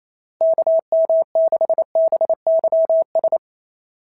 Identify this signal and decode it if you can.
Morse code, heard as a keyed tone, about 27 wpm, 690 Hz. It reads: KM6BYS
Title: SOTA QSO Practice 28wpm